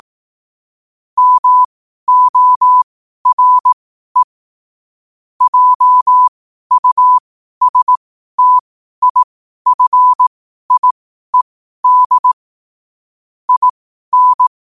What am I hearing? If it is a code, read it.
Morse code, heard as a keyed tone, about 18 wpm, spaced as for 12 wpm, 1000 Hz. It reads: MORE JUSTIFIED IN
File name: qotdfiles / 12wpm